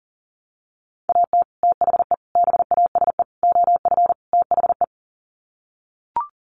Morse code, heard as a keyed tone, about 37 wpm, 700 Hz. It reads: AT THE BASE OF THE
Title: Sets of 5 Words 40wpm